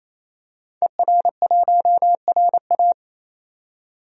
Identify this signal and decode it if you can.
Morse code, heard as a keyed tone, about 28 wpm, 700 Hz. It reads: ER1RA
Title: DXCC-100 Call Signs - 5 Characters - 28wpm